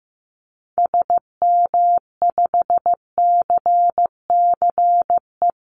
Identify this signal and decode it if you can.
Morse code, heard as a keyed tone, about 15 wpm, 700 Hz. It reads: SM5CCE
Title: Call Signs Repeated 3x - 15wpm